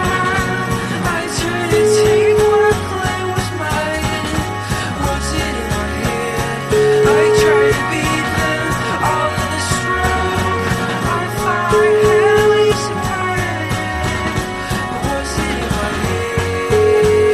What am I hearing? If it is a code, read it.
Ringback tone (Europe, 425 Hz), with noise.